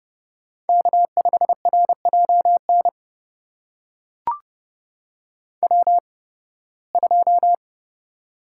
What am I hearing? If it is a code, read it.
Morse code, heard as a keyed tone, about 29 wpm, spaced as for 19 wpm, 700 Hz. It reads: K5RJN W2
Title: USA Call Signs - 5 Characters - 8x Character Spacing - 30wpm